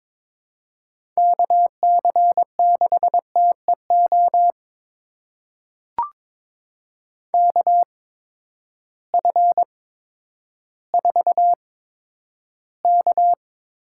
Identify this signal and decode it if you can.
Morse code, heard as a keyed tone, about 23 wpm, spaced as for 11 wpm, 700 Hz. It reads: KC6TEO KF4K
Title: USA Call Signs - 6 Characters - 8x Character Spacing - 22wpm